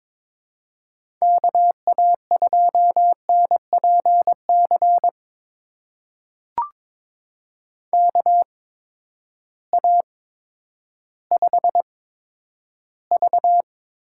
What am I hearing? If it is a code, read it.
Morse code, heard as a keyed tone, about 22 wpm, spaced as for 11 wpm, 700 Hz. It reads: KA2NPC KA5V
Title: USA Call Signs - 6 Characters - 8x Character Spacing - 22wpm